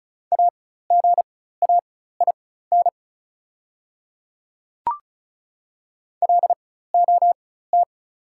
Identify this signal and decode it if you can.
Morse code, heard as a keyed tone, about 34 wpm, spaced as for 16 wpm, 690 Hz. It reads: AGAIN LOT
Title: 4x Character Spacing - Top 500 Words - 35wpm